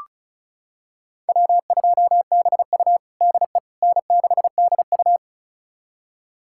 Morse code, heard as a keyed tone, about 36 wpm, 700 Hz. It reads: W2BU DE N6DU
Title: USA Callsign DE Callsign - 4 Characters - 35wpm